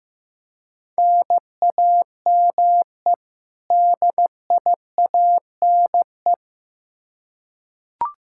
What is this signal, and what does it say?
Morse code, heard as a keyed tone, about 15 wpm, 700 Hz. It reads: NAME DIANE
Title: K1USN SST QSO Practice 15wpm